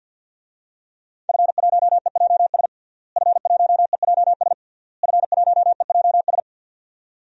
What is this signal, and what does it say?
Morse code, heard as a keyed tone, about 50 wpm, 700 Hz. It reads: F1EJS F1EJS F1EJS
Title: DXCC-100 Call Signs - 5 Characters - Repeated 3x - 50wpm